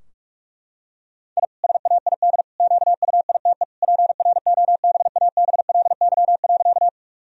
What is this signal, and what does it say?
Morse code, heard as a keyed tone, about 46 wpm, 700 Hz. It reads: I SAID QUITE PROBABLY.
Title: Sentences from Top 1000 Words - Review Part 4 - 45wpm